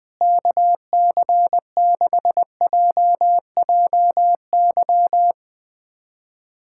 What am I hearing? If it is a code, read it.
Morse code, heard as a keyed tone, about 20 wpm, 700 Hz. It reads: KC6JJY
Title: USA Callsign DE Callsign 20wpm